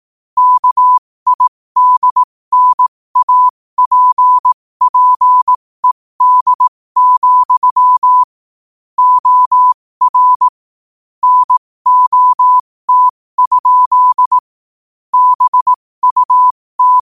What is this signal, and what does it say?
Morse code, heard as a keyed tone, about 18 wpm, spaced as for 15 wpm, 1000 Hz. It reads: KIDNAPPED, OR NOT? BUT
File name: qotdfiles / 15wpm